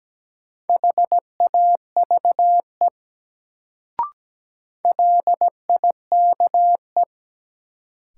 Morse code, heard as a keyed tone, about 17 wpm, 700 Hz. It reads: HAVE LIKE
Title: Sentences from Top 100 Words - New Word Drill - 17wpm